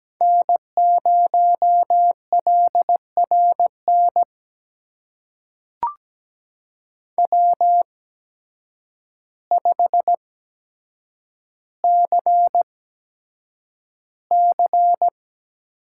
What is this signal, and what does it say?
Morse code, heard as a keyed tone, about 17 wpm, spaced as for 6 wpm, 700 Hz. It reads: N0LR NW5CC
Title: USA Call Signs - 8x Character Spacing - 17wpm